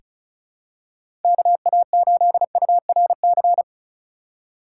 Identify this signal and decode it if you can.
Morse code, heard as a keyed tone, about 34 wpm, 700 Hz. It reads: KA8URC
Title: USA Call Signs - 6 Characters - 3x Character Spacing - 35wpm